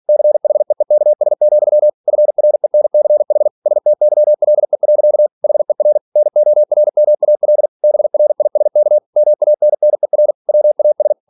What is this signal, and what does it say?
Morse code, heard as a keyed tone, about 46 wpm, 600 Hz. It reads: CHEEKS, FRENCH STYLE. HER NORMAL BRISK MANNER WAS